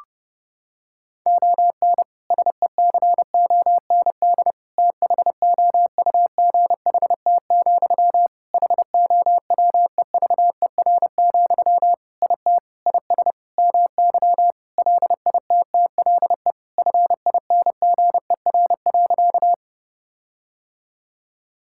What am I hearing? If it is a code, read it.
Morse code, heard as a keyed tone, about 30 wpm, 700 Hz. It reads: ON SECOND THOUGHT, HOWEVER, IT IS MY LITTLE FINGER.
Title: Sentences from Top 1600 Words 30wpm